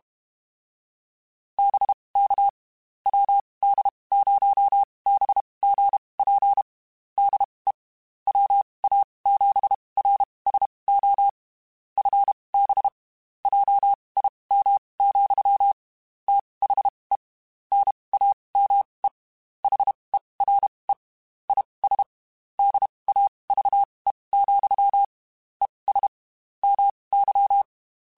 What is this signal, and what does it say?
Morse code, heard as a keyed tone, about 32 wpm, spaced as for 23 wpm, 800 Hz. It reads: BK WD0BGP DE WA7RSO FB JIM, THE NAME HERE IS DAVE, ES MY
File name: QSOs / QSOs-mp3s / WD0BGP 1@30